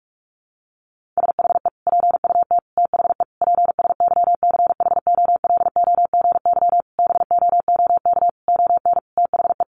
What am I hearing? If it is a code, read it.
Morse code, heard as a keyed tone, about 44 wpm, 700 Hz. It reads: SHE PUT THE PSYCHOLOGY BOOK ON THE
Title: Sentences from Top 1400 Words 45wpm